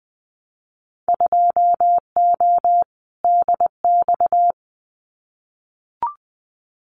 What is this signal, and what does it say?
Morse code, heard as a keyed tone, about 20 wpm, 700 Hz. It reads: 2O DX
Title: Winter Field Day Exchanges Encore 20wpm